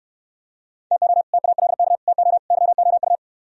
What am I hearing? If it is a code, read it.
Morse code, heard as a keyed tone, about 68 wpm, 700 Hz. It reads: TO TALK TO YOU